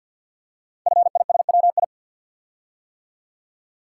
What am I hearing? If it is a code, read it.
Morse code, heard as a keyed tone, about 48 wpm, 700 Hz. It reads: FISWI
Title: Code Groups - 5 Letters - 50wpm